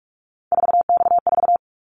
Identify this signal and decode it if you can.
Morse code, heard as a keyed tone, about 46 wpm, 690 Hz. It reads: 4X4
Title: Callsign Prefixes for Top 50 DXCC - 45wpm